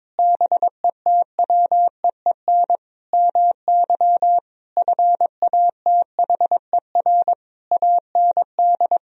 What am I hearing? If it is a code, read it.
Morse code, heard as a keyed tone, about 22 wpm, 700 Hz. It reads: BETWEEN MY FATHER AND
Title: Sentences from Top 1000 Words - Review Part 3 - 22wpm